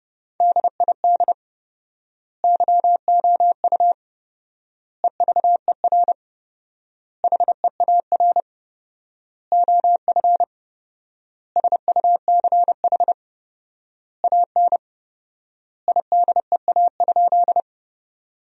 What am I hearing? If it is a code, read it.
Morse code, heard as a keyed tone, about 30 wpm, 700 Hz. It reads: DID YOU EVER HEAR OF SUCH AN IDEA?
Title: 4x Word-Spacing - Sentences from Top 400 Words - 30wpm